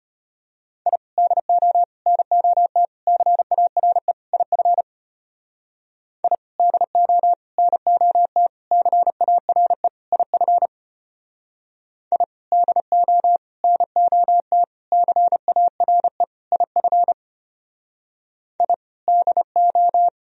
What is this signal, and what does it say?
Morse code, heard as a keyed tone, about 33 wpm, 700 Hz. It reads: I DO NOT CARE IF I DO NOT CARE IF I DO NOT CARE IF I DO